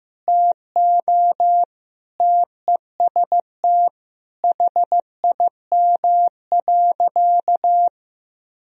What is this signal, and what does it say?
Morse code, heard as a keyed tone, about 15 wpm, 700 Hz. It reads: TO TEST HIM.
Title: Sentences from Top 1000 Words - Review Part 3 - 15wpm